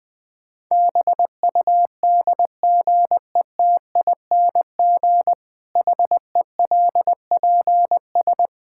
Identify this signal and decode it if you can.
Morse code, heard as a keyed tone, about 20 wpm, 700 Hz. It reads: BUDGETING HELPS
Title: Difficult Sentences with Hints - 20wpm